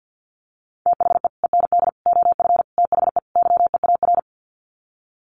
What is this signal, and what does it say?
Morse code, heard as a keyed tone, about 50 wpm, 700 Hz. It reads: THE END OF THE YEAR
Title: Sets of 5 Words 50wpm